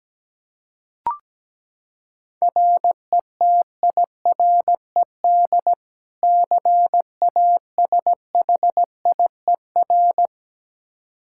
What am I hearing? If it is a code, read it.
Morse code, heard as a keyed tone, about 17 wpm, 700 Hz. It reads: RETIRED CASHIER
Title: K1USN SST QSO Practice 17wpm